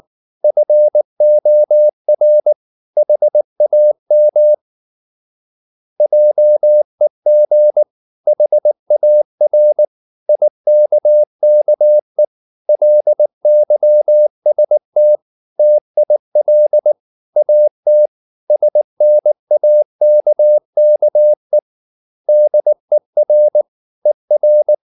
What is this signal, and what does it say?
Morse code, heard as a keyed tone, about 19 wpm, 600 Hz. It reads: FOR HAM JEG HAR IKKE LYST TIL AT SNAKKE DER ER
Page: Ferie 19 wpm